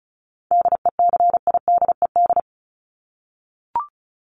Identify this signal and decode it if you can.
Morse code, heard as a keyed tone, about 34 wpm, 700 Hz. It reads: DECIDED
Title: ED Suffix - Top 100 Words with Base Words 35wpm